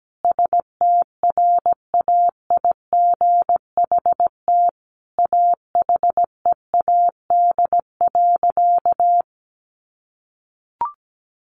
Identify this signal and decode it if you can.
Morse code, heard as a keyed tone, about 17 wpm, 700 Hz. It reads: STRAIGHT AHEAD.
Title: Sentences from Top 1100 Words 17wpm